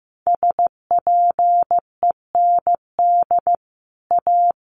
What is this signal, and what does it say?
Morse code, heard as a keyed tone, about 15 wpm, 700 Hz. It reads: SPEND A
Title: Sets of 5 Words 15wpm